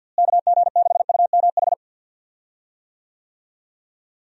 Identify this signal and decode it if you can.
Morse code, heard as a keyed tone, about 51 wpm, 700 Hz. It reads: KG6UMH
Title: USA Call Signs - 6 Characters - Repeated 2x - 50wpm